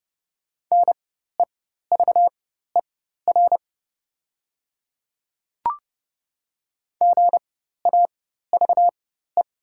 Morse code, heard as a keyed tone, about 30 wpm, spaced as for 14 wpm, 700 Hz. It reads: NEVER GAVE
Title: 4x Character Spacing - Top 500 Words - 30wpm